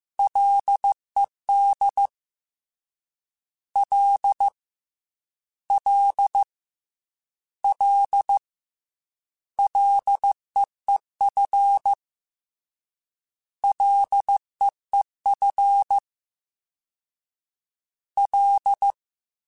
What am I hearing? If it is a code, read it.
Morse code, heard as a keyed tone, about 15 wpm, 800 Hz. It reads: LED L L L LEEF LEEF L
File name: Ls / Single L m3s / Single L @13